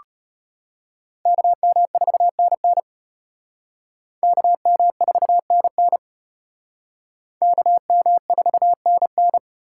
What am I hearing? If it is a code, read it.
Morse code, heard as a keyed tone, about 32 wpm, 700 Hz. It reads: KM4NN KM4NN KM4NN